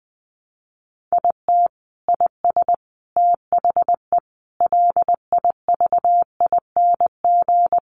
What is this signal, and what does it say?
Morse code, heard as a keyed tone, about 19 wpm, 700 Hz. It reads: IT IS THE LIVING
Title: Sentences from Top 1100 Words 20wpm